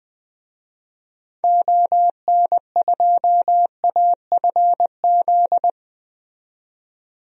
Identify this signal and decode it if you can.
Morse code, heard as a keyed tone, about 20 wpm, 700 Hz. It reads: ON2AFZ
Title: DXCC-100 Call Signs - 6 Characters - 20wpm